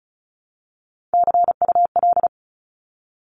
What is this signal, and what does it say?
Morse code, heard as a keyed tone, about 34 wpm, 700 Hz. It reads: CUL
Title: Abbreviations 35wpm